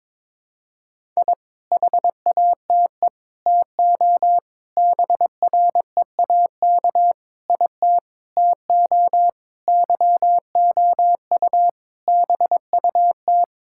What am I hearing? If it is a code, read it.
Morse code, heard as a keyed tone, about 22 wpm, 700 Hz. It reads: I HATE TO BREAK IT TO YOU BUT
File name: practice / 22wpm / Sentences with 8-Word Phrases - 22wpm